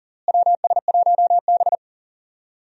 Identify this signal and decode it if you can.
Morse code, heard as a keyed tone, about 40 wpm, 700 Hz. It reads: WS1B
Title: USA Call Signs - 4 Characters - Repeated 2x - 40wpm